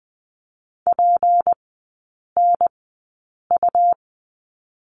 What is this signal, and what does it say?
Morse code, heard as a keyed tone, about 20 wpm, spaced as for 8 wpm, 690 Hz. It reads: PNU
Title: 166 Lesson 27e - Number 7 - Three Characters 20wpm